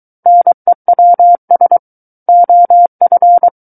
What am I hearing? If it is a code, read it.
Morse code, heard as a keyed tone, about 23 wpm, 700 Hz. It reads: NEWS OF
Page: SKCC Learning Center - Morse Code Practice Files